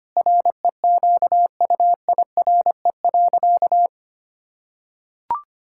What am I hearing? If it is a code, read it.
Morse code, heard as a keyed tone, about 25 wpm, 700 Hz. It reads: REQUIRE.
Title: Sentences from Top 1000 Words - Review Part 6 - 25wpm